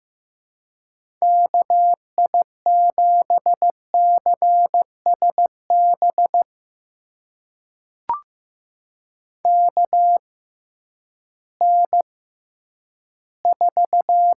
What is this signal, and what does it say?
Morse code, heard as a keyed tone, about 15 wpm, spaced as for 10 wpm, 700 Hz. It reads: KI7CSB KN4
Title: USA Call Signs - 6 Characters - 6x Character Spacing - 15wpm